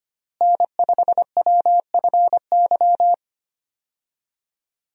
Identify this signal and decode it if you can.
Morse code, heard as a keyed tone, about 25 wpm, 700 Hz. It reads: N5WFY
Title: USA Call Signs - 5 Characters - Repeated 3x - 25wpm